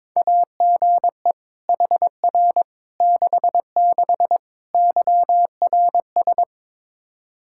Morse code, heard as a keyed tone, about 22 wpm, 700 Hz. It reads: AGE HR 66 YRS
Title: Practice QSO - Exchange 3 - More Info - 22wpm